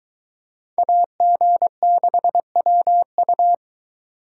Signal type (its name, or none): Morse code